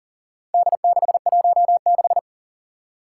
Morse code, heard as a keyed tone, about 40 wpm, 700 Hz. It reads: D616
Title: Four Challenging Letter-Number Combinations 40wpm